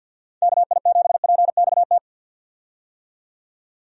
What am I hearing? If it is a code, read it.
Morse code, heard as a keyed tone, about 50 wpm, 700 Hz. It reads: KI7PXT
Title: USA Call Signs - 6 Characters - 50wpm